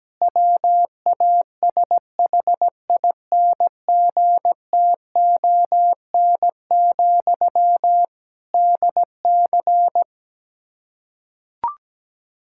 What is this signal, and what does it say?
Morse code, heard as a keyed tone, about 17 wpm, 700 Hz. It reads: WASHINGTON, DC
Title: Practice QSO - Exchange 1 - Beginning - 17wpm